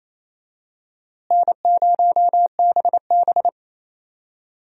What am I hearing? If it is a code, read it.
Morse code, heard as a keyed tone, about 28 wpm, 700 Hz. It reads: N0BB
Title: USA Call Signs - 4 Characters - Repeated 3x - 28wpm